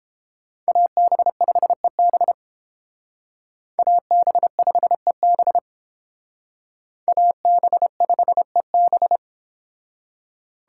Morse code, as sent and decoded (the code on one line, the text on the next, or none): .- -... ..... -... / .- -... ..... -... / .- -... ..... -...
AB5EB AB5EB AB5EB